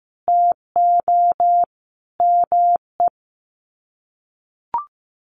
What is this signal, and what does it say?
Morse code, heard as a keyed tone, about 14 wpm, 700 Hz. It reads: TO ME
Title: Sets of 5 Words Encore2 15wpm